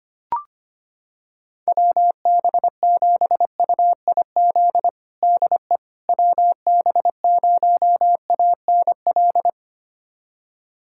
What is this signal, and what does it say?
Morse code, heard as a keyed tone, about 25 wpm, 700 Hz. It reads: WB7UIZ DE WB0ANL
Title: USA Callsign DE Callsign - 6 Characters - 25wpm